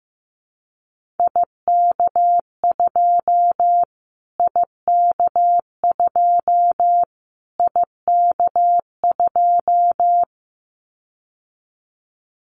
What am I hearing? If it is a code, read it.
Morse code, heard as a keyed tone, about 15 wpm, 700 Hz. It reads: IK2 IK2 IK2
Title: Callsign Prefixes for Top 25 DXCC - Repeated 3x - 15wpm